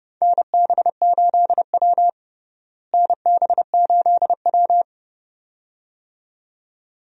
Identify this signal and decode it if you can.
Morse code, heard as a keyed tone, about 30 wpm, 700 Hz. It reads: NB8W NB8W
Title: USA Call Signs - 4 Characters - Repeated 2x - 30wpm